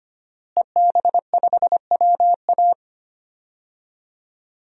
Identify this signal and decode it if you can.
Morse code, heard as a keyed tone, about 25 wpm, 700 Hz. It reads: EB5WA
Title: DXCC-100 Call Signs - 5 Characters - Repeated 3x - 25wpm